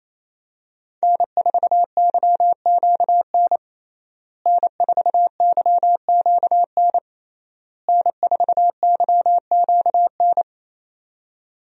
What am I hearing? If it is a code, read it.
Morse code, heard as a keyed tone, about 28 wpm, 700 Hz. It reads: N4YQN N4YQN N4YQN
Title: USA Call Signs - 5 Characters - Repeated 3x - 28wpm